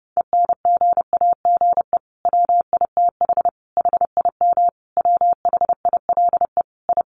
Morse code, heard as a keyed tone, about 30 wpm, 700 Hz. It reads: ENGAGE WITH HIM WHILE I
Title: Sentences from Top 1500 Words 30wpm